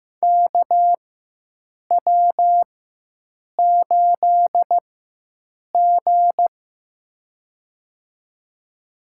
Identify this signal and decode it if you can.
Morse code, heard as a keyed tone, about 15 wpm, spaced as for 7 wpm, 700 Hz. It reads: KW8G
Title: USA Call Signs - 4x Character Spacing - 15wpm